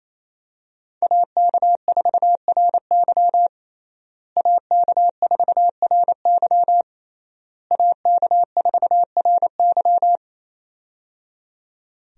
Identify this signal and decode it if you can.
Morse code, heard as a keyed tone, about 27 wpm, 700 Hz. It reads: AK4RY AK4RY AK4RY
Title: USA Call Signs - 5 Characters - Repeated 3x - 28wpm